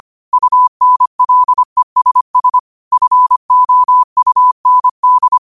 Morse code, heard as a keyed tone, about 25 wpm, 1000 Hz. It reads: UNLESS FOUND
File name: qotdfiles / 25wpm